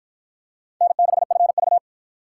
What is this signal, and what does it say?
Morse code, heard as a keyed tone, about 47 wpm, 700 Hz. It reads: N6FV